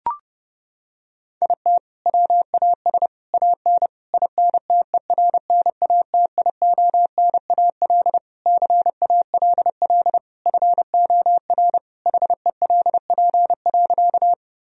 Morse code, heard as a keyed tone, about 30 wpm, 700 Hz. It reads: IT WAS AN INTERNATIONAL CALL FOR HELP.
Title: Sentences from Top 1000 Words - Review Part 4 - 30wpm